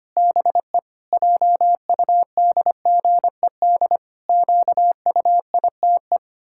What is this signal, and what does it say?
Morse code, heard as a keyed tone, about 25 wpm, 700 Hz. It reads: BE JUDGED QUITE
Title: Pangram Challenge v2 - 25wpm